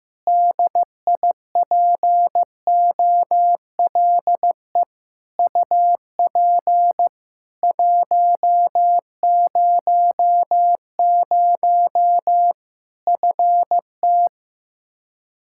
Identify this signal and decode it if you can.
Morse code, heard as a keyed tone, about 16 wpm, 700 Hz. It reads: DIPOLE UP 100 FT
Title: Practice QSO - Exchange 2 - Info - 15wpm